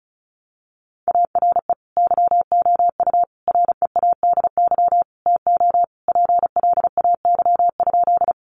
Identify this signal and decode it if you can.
Morse code, heard as a keyed tone, about 34 wpm, 700 Hz. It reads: ARE YOU READY TO PLAY?
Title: Sentences from Top 400 Words 35wpm